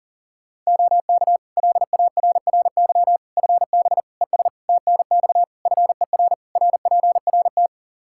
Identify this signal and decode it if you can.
Morse code, heard as a keyed tone, about 40 wpm, 700 Hz. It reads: OK LARRY FB ES TNX FER RPRT
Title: Practice QSO - Exchange 2 - Info - 40wpm